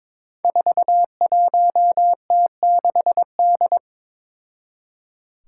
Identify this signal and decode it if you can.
Morse code, heard as a keyed tone, about 23 wpm, 700 Hz. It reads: 41T6D
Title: Five Challenging Letter-Number Combinations 22wpm